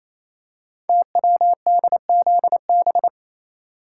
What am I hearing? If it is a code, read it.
Morse code, heard as a keyed tone, about 29 wpm, 700 Hz. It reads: TWDZB
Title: Code Groups - 5 Letters - 28wpm